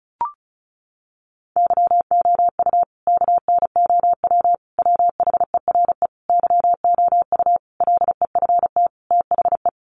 Morse code, heard as a keyed tone, about 34 wpm, 700 Hz. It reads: YOU KNOW WHERE YOU LEFT THE
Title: Sentences from Top 1500 Words 35wpm